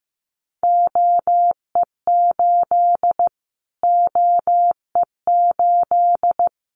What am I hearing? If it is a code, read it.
Morse code, heard as a keyed tone, about 15 wpm, 700 Hz. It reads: OE8 OE8
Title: Callsign Prefixes for Top 25 DXCC - Repeated 3x - 15wpm